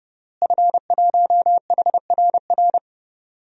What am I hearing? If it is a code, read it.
Morse code, heard as a keyed tone, about 32 wpm, 700 Hz. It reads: F1HRR
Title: DXCC-100 Call Signs - 5 Characters - Repeated 3x - 30wpm